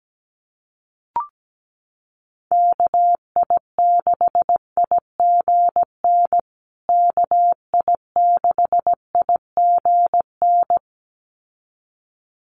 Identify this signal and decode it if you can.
Morse code, heard as a keyed tone, about 17 wpm, 700 Hz. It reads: KI6IGN KI6IGN
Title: USA Call Signs Repeated 2x 17wpm